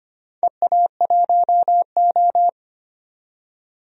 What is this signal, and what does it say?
Morse code, heard as a keyed tone, about 25 wpm, 700 Hz. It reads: EA1O
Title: DXCC-100 Call Signs - 4 Characters - 25wpm